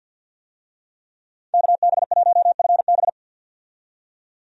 Morse code, heard as a keyed tone, about 51 wpm, 700 Hz. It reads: KB1FB
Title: USA Call Signs - 5 Characters - 50wpm